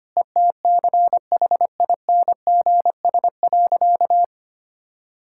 Morse code, heard as a keyed tone, about 25 wpm, 700 Hz. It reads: ETCHINGS.
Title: Pangram Challenge v2 - 25wpm